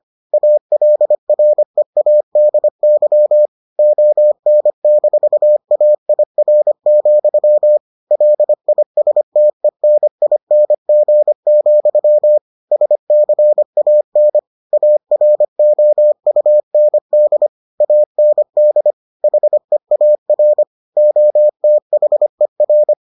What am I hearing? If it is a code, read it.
Morse code, heard as a keyed tone, about 25 wpm, 600 Hz. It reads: ALREADY ON-AIR, LISTENING, SCAN AROUND AND HEAR OTHER